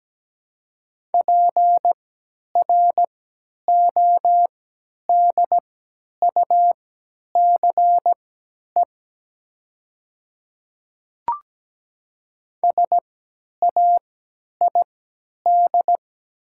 Morse code, heard as a keyed tone, about 17 wpm, spaced as for 10 wpm, 700 Hz. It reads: PRODUCE SAID
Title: 3x Character Spacing - Top 500 Words - 17wpm